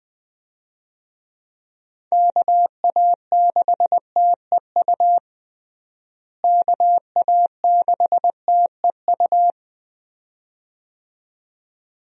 Morse code, heard as a keyed tone, about 19 wpm, 700 Hz. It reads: KA6TEU KA6TEU
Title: USA Call Signs - 6 Characters - Repeated 2x - 20wpm